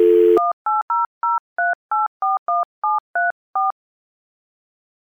DTMF, heard as telephone touch tones, tones of about 180 ms. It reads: 18003841734